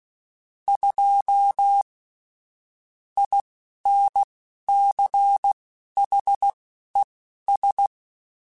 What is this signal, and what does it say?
Morse code, heard as a keyed tone, about 16 wpm, spaced as for 11 wpm, 800 Hz. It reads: 2 INCHES
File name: Weather Sets / Weather Sets-mp3s / Weather 3 17 @10